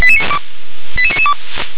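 Five-tone selective call (ZVEI), heard as quick runs of five tones, tones of about 70 ms. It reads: 88992
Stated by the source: ZVEI five-tone selective call received by SDR.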